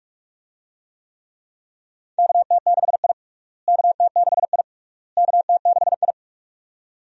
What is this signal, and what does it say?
Morse code, heard as a keyed tone, about 46 wpm, 700 Hz. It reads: KT6I KT6I KT6I
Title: USA Call Signs - 4 Characters - Repeated 3x - 45wpm